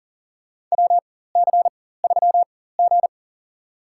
Morse code, heard as a keyed tone, about 37 wpm, spaced as for 18 wpm, 700 Hz. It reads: WC3G
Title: USA Call Signs - 4 Characters - 4x Character Spacing - 40wpm